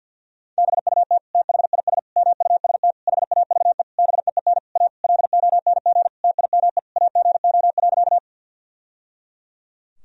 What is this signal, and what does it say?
Morse code, heard as a keyed tone, about 50 wpm, 700 Hz. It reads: BUT THIS MUST HAVE BEEN A LONG TIME AGO.
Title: Sentences from Top 1000 Words - Review Part 3 - 50wpm